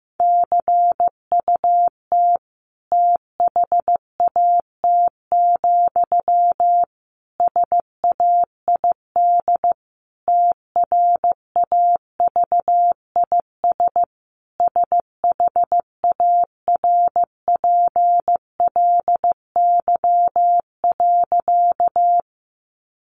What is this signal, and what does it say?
Morse code, heard as a keyed tone, about 15 wpm, 700 Hz. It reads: CUT THAT, SAID TRAVIS SHARPLY.
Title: A Sound Of Thunder 15wpm